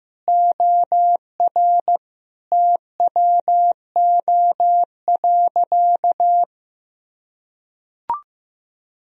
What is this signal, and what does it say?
Morse code, heard as a keyed tone, about 15 wpm, 700 Hz. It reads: OR TWO.
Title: Sentences from Top 1000 Words - Review Part 1 - 15wpm